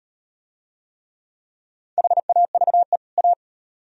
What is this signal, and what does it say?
Morse code, heard as a keyed tone, about 38 wpm, 700 Hz. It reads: HAVE A